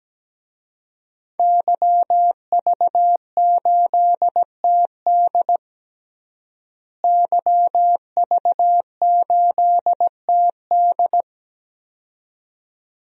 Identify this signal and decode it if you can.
Morse code, heard as a keyed tone, about 17 wpm, 700 Hz. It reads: YV8TD YV8TD
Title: DXCC-100 Call Signs - 5 Characters - Repeated 2x - 17wpm